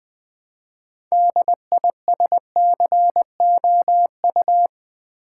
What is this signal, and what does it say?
Morse code, heard as a keyed tone, about 20 wpm, 700 Hz. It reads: DISCOU
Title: Build A Word - 3 Letters At A Time - 8-Letter Words - 20wpm